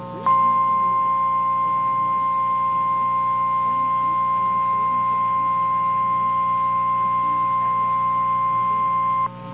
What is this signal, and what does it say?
Emergency Alert System (SAME) tones (the 1050 Hz attention tone). It is heard from 0.2 to 9.3 s.
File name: eas / KNCYKBIE / KNCYKBIE